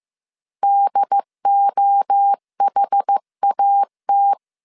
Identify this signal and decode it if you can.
Morse code, heard as a keyed tone, about 15 wpm, 800 Hz. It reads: DOHAT
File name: CW / DET / 1 3 1500C S A 14wpm 2447